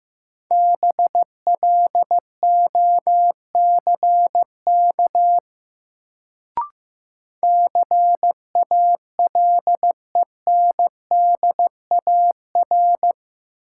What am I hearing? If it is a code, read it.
Morse code, heard as a keyed tone, about 15 wpm, 700 Hz. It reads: BLOCK CALENDAR
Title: Sentences from Top 1600 Words - New Word Drill - 15wpm